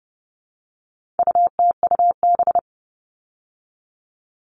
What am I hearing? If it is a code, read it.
Morse code, heard as a keyed tone, about 30 wpm, 700 Hz. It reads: UTUB
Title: All Cut Numbers - 4-Digit 30wpm